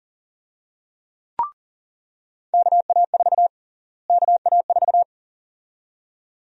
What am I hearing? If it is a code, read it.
Morse code, heard as a keyed tone, about 40 wpm, 700 Hz. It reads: KA4 KA4
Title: USA Call Sign Prefixes - Repeated 2x - 40wpm